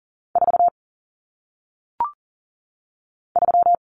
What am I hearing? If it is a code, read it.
Morse code, heard as a keyed tone, about 39 wpm, 700 Hz. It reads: 4 3
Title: Numbers - 1-Digit 40wpm